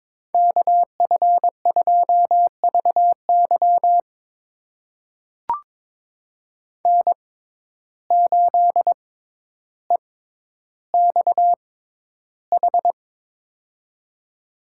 Morse code, heard as a keyed tone, about 22 wpm, spaced as for 11 wpm, 700 Hz. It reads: KF2VY N8EXH
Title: USA Call Signs - 5 Characters - 6x Character Spacing - 22wpm